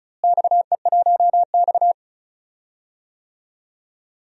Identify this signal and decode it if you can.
Morse code, heard as a keyed tone, about 36 wpm, 700 Hz. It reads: XE1X
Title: DXCC-100 Call Signs - 4 Characters - Repeated 3x - 35wpm